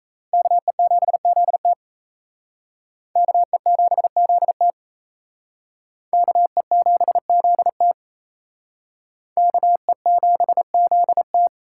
Morse code, heard as a keyed tone, about 36 wpm, 700 Hz. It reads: KE7ZT KE7ZT KE7ZT KE7ZT